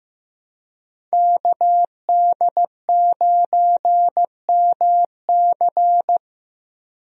Morse code, heard as a keyed tone, about 15 wpm, 700 Hz. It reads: KD9MC
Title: USA Call Signs - 5 Characters - 15wpm